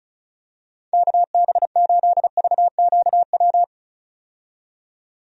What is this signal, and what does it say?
Morse code, heard as a keyed tone, about 36 wpm, 700 Hz. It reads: KB8VQW
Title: USA Call Signs - 6 Characters - 35wpm